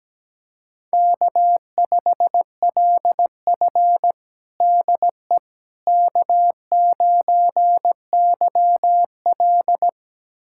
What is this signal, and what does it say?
Morse code, heard as a keyed tone, about 17 wpm, 700 Hz. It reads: K5LF DE K9YL
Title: USA Callsign DE Callsign - 4 Characters - 17wpm